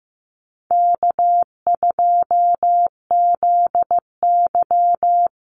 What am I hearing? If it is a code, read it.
Morse code, heard as a keyed tone, about 15 wpm, 700 Hz. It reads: K2ZY
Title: USA Call Signs Repeated 2x 15wpm